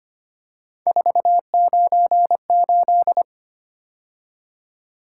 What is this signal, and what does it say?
Morse code, heard as a keyed tone, about 29 wpm, 700 Hz. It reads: 498
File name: Speed-Racing - Three Letter-Number Combinations - 25wpm